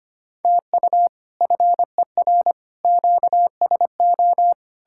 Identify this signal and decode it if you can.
Morse code, heard as a keyed tone, about 33 wpm, 700 Hz. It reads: TU FER QSO